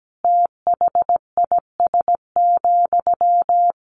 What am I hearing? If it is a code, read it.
Morse code, heard as a keyed tone, about 17 wpm, 700 Hz. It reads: THIS,
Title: Sentences from Top 1400 Words 17wpm